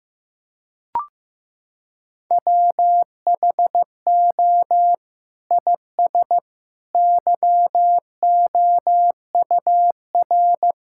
Morse code, heard as a keyed tone, about 15 wpm, 700 Hz. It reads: WHO IS YOUR
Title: Sentences from Top 1600 Words 15wpm